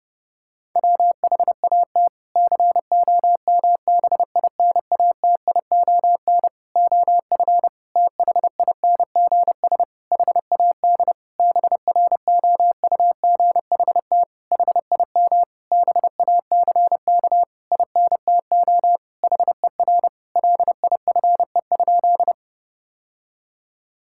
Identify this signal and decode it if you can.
Morse code, heard as a keyed tone, about 32 wpm, 700 Hz. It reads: WHAT COMBINATION OF THINGS HAD BROUGHT HIM BACK INTO HER LIFE?
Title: Sentences from Top 1000 Words - Review Part 6 - 30wpm